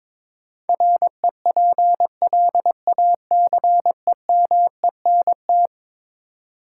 Morse code, heard as a keyed tone, about 22 wpm, 700 Hz. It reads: REPLACEMENT
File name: practice / 22wpm / Eleven Letter Words - 22wpm